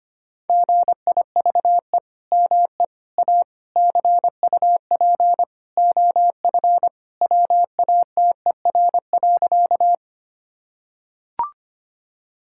Morse code, heard as a keyed tone, about 25 wpm, 700 Hz. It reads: GIVE ME A CUP OF WATER.
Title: Sentences from Top 1600 Words 25wpm